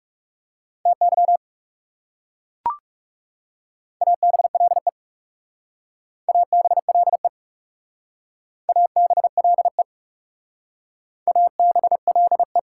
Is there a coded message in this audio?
Yes: Morse code from 0.9 to 12.6 s, heard as a keyed tone, about 39 wpm, 700 Hz. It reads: TY ABLE ABLE ABLE ABLE